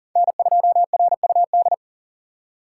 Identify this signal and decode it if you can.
Morse code, heard as a keyed tone, about 40 wpm, 700 Hz. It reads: N2RUD
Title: USA Call Signs - 5 Characters - 3x Character Spacing - 40wpm